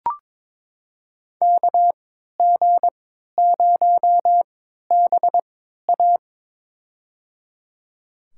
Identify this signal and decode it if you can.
Morse code, heard as a keyed tone, about 21 wpm, spaced as for 12 wpm, 700 Hz. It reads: KG0BA
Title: USA Call Signs - 5 Characters - 3x Character Spacing - 22wpm